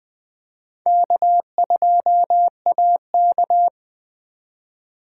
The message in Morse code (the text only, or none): K2AK